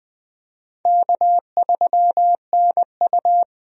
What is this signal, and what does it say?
Morse code, heard as a keyed tone, about 21 wpm, 700 Hz. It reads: K3NU
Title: USA Call Signs - 4 Characters - 20wpm